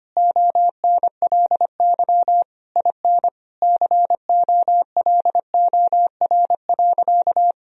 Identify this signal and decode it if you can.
Morse code, heard as a keyed tone, about 25 wpm, 700 Hz. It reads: ONLY IN COLOR.
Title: Sentences from Top 1000 Words - Review Part 2 - 25wpm